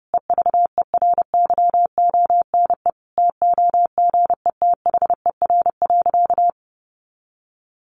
Morse code, heard as a keyed tone, about 30 wpm, 700 Hz. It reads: EVERYONE TOGETHER.
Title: Sentences from Top 1500 Words 30wpm